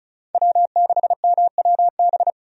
Morse code, heard as a keyed tone, about 36 wpm, 700 Hz. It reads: W6MWB
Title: USA Call Signs - 5 Characters - 35wpm